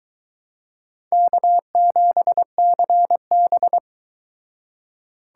Morse code, heard as a keyed tone, about 22 wpm, 700 Hz. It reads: K7CB